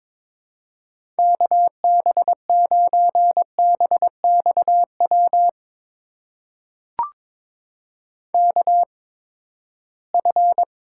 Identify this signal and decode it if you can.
Morse code, heard as a keyed tone, about 22 wpm, spaced as for 15 wpm, 700 Hz. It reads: KB9BXW KF
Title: USA Call Signs - 6 Characters - 8x Character Spacing - 22wpm